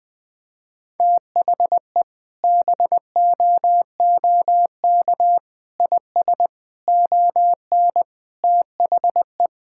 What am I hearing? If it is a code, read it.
Morse code, heard as a keyed tone, about 20 wpm, 700 Hz. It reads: THE BOOK IS ON THE
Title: Sentences with 3-Word Phrases - Spoken Beginning - 20wpm